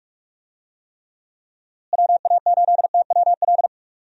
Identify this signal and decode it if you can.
Morse code, heard as a keyed tone, about 45 wpm, 700 Hz. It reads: WA8TWL
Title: USA Call Signs - 6 Characters - Repeated 3x - 45wpm